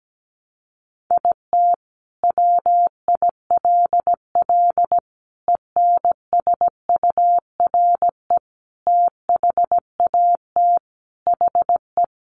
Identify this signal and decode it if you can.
Morse code, heard as a keyed tone, about 17 wpm, 700 Hz. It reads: IT WILL ENSURE THAT HE
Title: Sentences from Top 1100 Words 17wpm